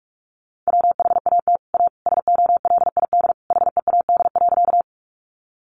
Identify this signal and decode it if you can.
Morse code, heard as a keyed tone, about 44 wpm, 700 Hz. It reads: WHAT A SOLID HEAD.
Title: Sentences from Top 1500 Words 45wpm